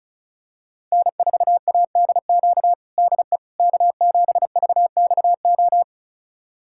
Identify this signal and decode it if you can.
Morse code, heard as a keyed tone, about 34 wpm, 700 Hz. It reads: N4ADQ DE K7VXO
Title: USA Callsign DE Callsign - 5 Characters - 35wpm